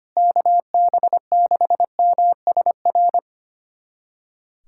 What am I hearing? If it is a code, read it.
Morse code, heard as a keyed tone, about 25 wpm, 700 Hz. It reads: KB6MSR
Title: USA Callsign DE Callsign - 6 Characters - 25wpm